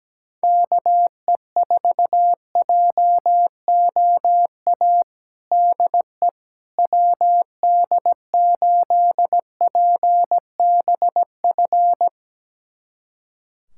Morse code, heard as a keyed tone, about 17 wpm, 700 Hz. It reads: KE4JOA DE WD8PBF
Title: USA Callsign DE Callsign - 6 Characters - 17wpm